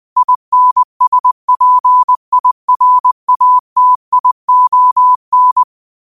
Morse code, heard as a keyed tone, about 20 wpm, 1000 Hz. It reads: INSPIRATION
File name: qotdfiles / 20wpm